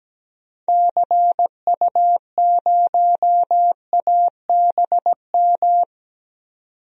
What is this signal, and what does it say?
Morse code, heard as a keyed tone, about 17 wpm, 700 Hz. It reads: CU0ABM
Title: DXCC-100 Call Signs - 6 Characters - 17wpm